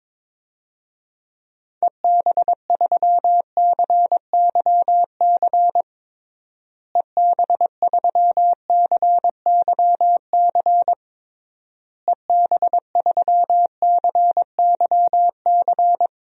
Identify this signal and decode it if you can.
Morse code, heard as a keyed tone, about 22 wpm, 700 Hz. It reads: EB3CYC EB3CYC EB3CYC
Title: DXCC-100 Call Signs - 6 Characters - Repeated 3x - 22wpm